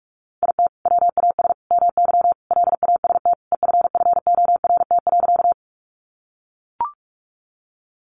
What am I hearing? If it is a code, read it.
Morse code, heard as a keyed tone, about 44 wpm, 700 Hz. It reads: IT WAS MY LAST EFFORT.
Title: Sentences from Top 1200 Words 45wpm